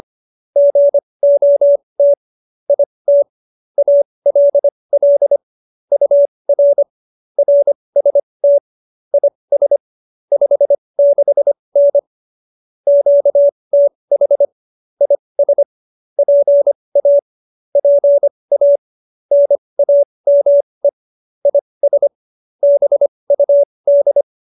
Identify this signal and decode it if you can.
Morse code, heard as a keyed tone, about 25 wpm, spaced as for 20 wpm, 570 Hz. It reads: GOT IT ALL UR RST IS 56N QTH IS PA PA NAME IS BUD
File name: QSO302 20